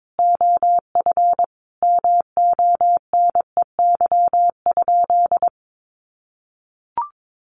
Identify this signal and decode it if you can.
Morse code, heard as a keyed tone, about 22 wpm, 700 Hz. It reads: OF MONEY?
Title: Sentences from Top 1100 Words 22wpm